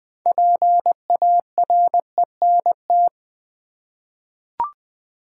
Morse code, heard as a keyed tone, about 20 wpm, 700 Hz. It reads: PARENT
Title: Six Letter Words - 20wpm